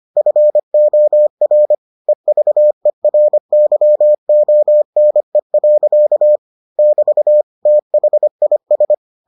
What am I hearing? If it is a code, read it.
Morse code, heard as a keyed tone, about 25 wpm, 600 Hz. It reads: FOR EVERYONE. <BT> THIS